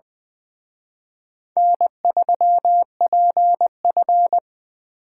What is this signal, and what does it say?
Morse code, heard as a keyed tone, about 24 wpm, 700 Hz. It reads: N3PF
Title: Speed Racing - USA 1x2 Call Signs 20wpm